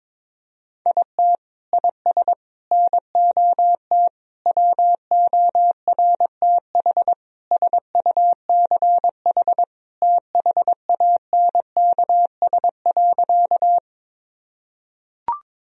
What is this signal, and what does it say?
Morse code, heard as a keyed tone, about 22 wpm, 700 Hz. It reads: IT IS NOT WORTH SUCH THANKS.
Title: Sentences from Top 1000 Words - Review Part 3 - 22wpm